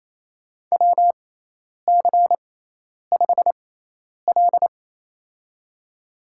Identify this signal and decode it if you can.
Morse code, heard as a keyed tone, about 28 wpm, spaced as for 10 wpm, 700 Hz. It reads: WC5L
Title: USA Call Signs - 4 Characters - 6x Character Spacing - 28wpm